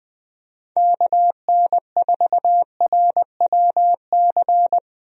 Morse code, heard as a keyed tone, about 21 wpm, 700 Hz. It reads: KN4RWC
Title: USA Call Signs - 6 Characters - Repeated 2x - 20wpm